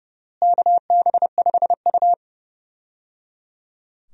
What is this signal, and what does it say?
Morse code, heard as a keyed tone, about 30 wpm, 700 Hz. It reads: KB5U
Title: USA Call Signs - 4 Characters - Repeated 3x - 30wpm